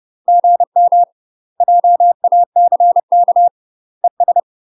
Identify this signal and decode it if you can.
Morse code, heard as a keyed tone, about 30 wpm, 700 Hz. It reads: GM JACK ES
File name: UK MASTER QSOs TRAINING/30WPM